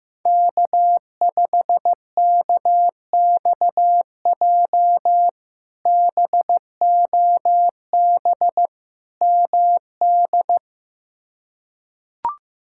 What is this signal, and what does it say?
Morse code, heard as a keyed tone, about 15 wpm, 700 Hz. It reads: K5KXJ BOB MD
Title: K1USN SST QSO Practice - 15wpm